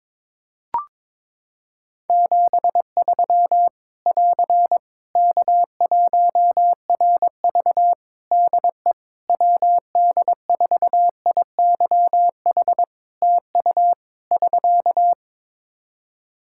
Morse code, heard as a keyed tone, about 22 wpm, 700 Hz. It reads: 73 <AR> K1RV DE WD4IYH TU <SK>
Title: Practice QSO - Exchange 4 - Ending - 22wpm